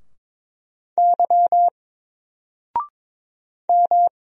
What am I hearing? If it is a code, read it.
Morse code, heard as a keyed tone, about 22 wpm, 700 Hz. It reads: Y M
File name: Single-Letter-Number-22wpm